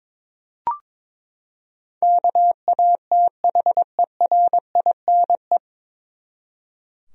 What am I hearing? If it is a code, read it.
Morse code, heard as a keyed tone, about 22 wpm, 700 Hz. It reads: KATHERINE
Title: USA Common Names - 22wpm